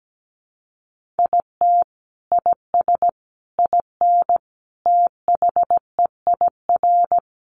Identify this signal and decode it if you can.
Morse code, heard as a keyed tone, about 17 wpm, 700 Hz. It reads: IT IS IN THEIR
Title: Sentences from Top 1300 Words 17wpm